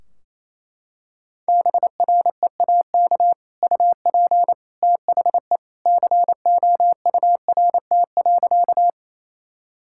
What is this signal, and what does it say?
Morse code, heard as a keyed tone, about 28 wpm, 700 Hz. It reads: BREAK UP THE COURT.
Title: Sentences from Top 1600 Words 28wpm